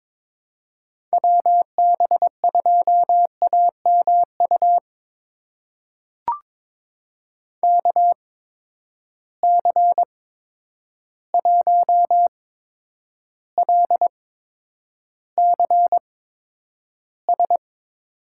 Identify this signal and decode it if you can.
Morse code, heard as a keyed tone, about 22 wpm, spaced as for 9 wpm, 700 Hz. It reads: WB2AMU KC1LCS
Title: USA Call Signs - 6 Characters - 8x Character Spacing - 22wpm